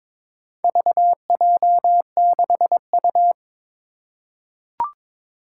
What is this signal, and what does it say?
Morse code, heard as a keyed tone, about 22 wpm, 700 Hz. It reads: VJ6U
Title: Four Challenging Letter-Number Combinations 22wpm